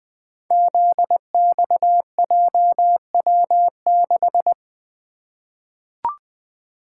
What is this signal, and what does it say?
Morse code, heard as a keyed tone, about 20 wpm, 700 Hz. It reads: ZXJW6
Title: Code Groups - 5 Letter:Numbers - 20wpm